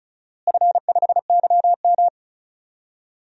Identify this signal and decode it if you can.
Morse code, heard as a keyed tone, about 36 wpm, 700 Hz. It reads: F5YM
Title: DXCC-100 Call Signs - 4 Characters - Repeated 2x - 35wpm